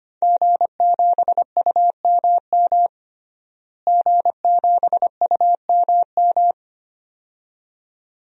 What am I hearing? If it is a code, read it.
Morse code, heard as a keyed tone, about 25 wpm, 700 Hz. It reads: G7UMM G7UMM
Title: DXCC-100 Call Signs - 5 Characters - Repeated 2x - 25wpm